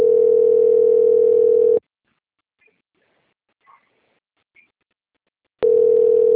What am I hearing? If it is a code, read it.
Ringback tone (North America), 440 + 480 Hz.